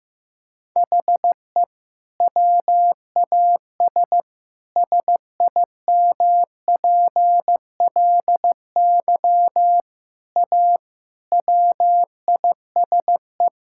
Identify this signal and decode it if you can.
Morse code, heard as a keyed tone, about 15 wpm, 700 Hz. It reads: HE WAS SIMPLY A WISE
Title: Sentences from Top 1600 Words 15wpm